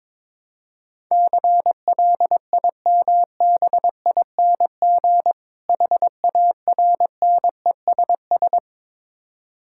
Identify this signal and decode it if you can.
Morse code, heard as a keyed tone, about 22 wpm, 690 Hz. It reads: CLIMBING HARNESS
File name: Difficult Words with Hints - 22wpm